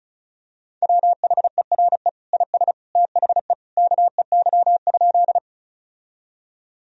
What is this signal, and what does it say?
Morse code, heard as a keyed tone, about 36 wpm, 700 Hz. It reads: WHERE IS THE KEY?
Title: Sentences from Top 1000 Words - Review Part 4 - 35wpm